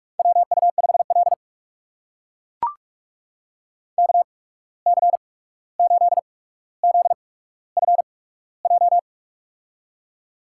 Morse code, heard as a keyed tone, about 44 wpm, spaced as for 17 wpm, 700 Hz. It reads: WU5L KC8ZFJ
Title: USA Call Signs - 8x Character Spacing - 45wpm